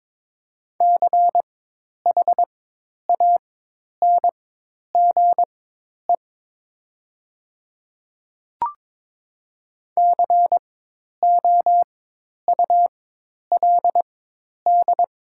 Morse code, heard as a keyed tone, about 22 wpm, spaced as for 10 wpm, 700 Hz. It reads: CHANGE COULD
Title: 4x Character Spacing - Top 200 Words - 22wpm